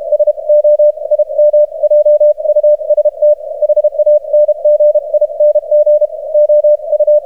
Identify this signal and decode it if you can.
Morse code, heard as a keyed tone, about 33 wpm, spaced as for 28 wpm, 600 Hz. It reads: SO IM JUST HANGING OU
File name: archived / CRSnet